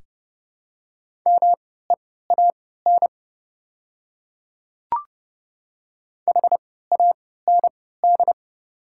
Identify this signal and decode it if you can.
Morse code, heard as a keyed tone, about 31 wpm, spaced as for 17 wpm, 690 Hz. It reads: MEAN HAND
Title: 3x Character Spacing - Top 200 Words - 30wpm